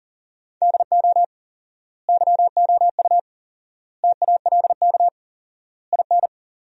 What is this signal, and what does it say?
Morse code, heard as a keyed tone, about 40 wpm, 700 Hz. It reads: DO YOU TALK IN
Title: 4x Word-Spacing - Sentences from Top 500 Words - 40wpm